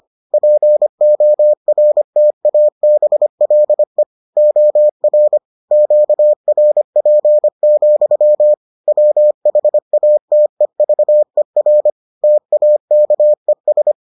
Morse code, heard as a keyed tone, about 25 wpm, 600 Hz. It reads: PORTABLE OR QRP, WHATEVER TAKES